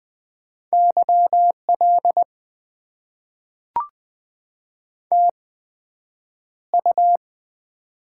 Morse code, heard as a keyed tone, about 20 wpm, spaced as for 9 wpm, 700 Hz. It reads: YL TU
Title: Top 90 QSO Elements - 8x Character Spacing - 20wpm